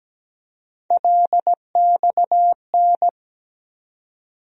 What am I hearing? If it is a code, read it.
Morse code, heard as a keyed tone, about 17 wpm, 700 Hz. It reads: LXN
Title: USA Call Sign Postfixes - 3 Letters 17wpm